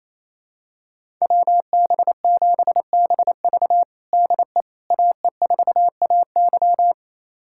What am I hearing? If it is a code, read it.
Morse code, heard as a keyed tone, about 28 wpm, 700 Hz. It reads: WB7BV DE AE4AY
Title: USA Callsign DE Callsign - 5 Characters - 28wpm